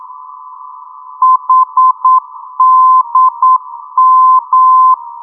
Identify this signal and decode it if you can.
Morse code, heard as a keyed tone, about 9 wpm, 1030 Hz. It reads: HDM
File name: europe / d-8023 - ndb coleman hdm-334 (2004) (rainer brannolte, d) xc